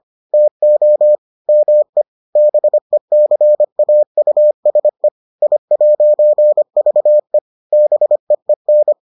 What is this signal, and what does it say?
Morse code, heard as a keyed tone, about 25 wpm, 600 Hz. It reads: TO ME BECAUSE I'VE BEEN